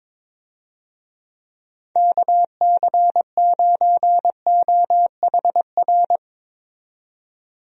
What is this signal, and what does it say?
Morse code, heard as a keyed tone, about 22 wpm, 700 Hz. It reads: KC9OHR
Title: USA Call Signs - 6 Characters - 22wpm